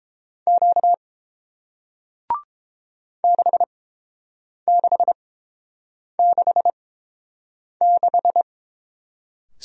Morse code, heard as a keyed tone, about 30 wpm, spaced as for 7 wpm, 700 Hz. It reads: Q 6666